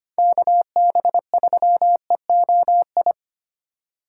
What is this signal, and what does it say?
Morse code, heard as a keyed tone, about 25 wpm, 700 Hz. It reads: KB3EOI
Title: USA Call Signs - 6 Characters - 2x Character Spacing - 25wpm